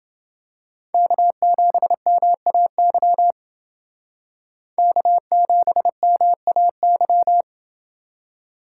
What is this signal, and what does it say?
Morse code, heard as a keyed tone, about 28 wpm, 700 Hz. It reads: K7MAY K7MAY